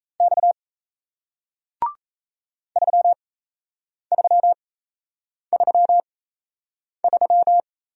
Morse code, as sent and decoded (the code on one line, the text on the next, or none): -..- / ...-- ...-- ...-- ...--
X 3333